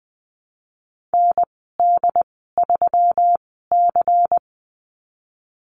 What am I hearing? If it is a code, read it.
Morse code, heard as a keyed tone, about 20 wpm, spaced as for 15 wpm, 700 Hz. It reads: ND3C
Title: 193 Lesson 31g - Letter v - Call Signs 20wpm